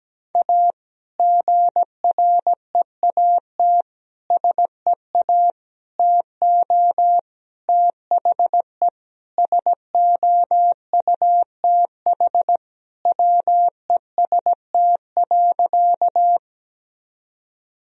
Morse code, heard as a keyed tone, about 17 wpm, 700 Hz. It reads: A GREAT SEA TO THE SOUTH WEST.
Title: Sentences from Top 1000 Words - Review Part 2 - 17wpm